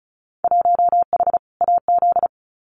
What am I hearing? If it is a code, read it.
Morse code, heard as a keyed tone, about 34 wpm, 700 Hz. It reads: 1H AZ
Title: Winter Field Day Exchanges 35wpm